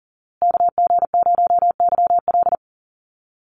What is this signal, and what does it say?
Morse code, heard as a keyed tone, about 40 wpm, 700 Hz. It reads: KG0YL
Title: Call Signs Repeated 2x - 40wpm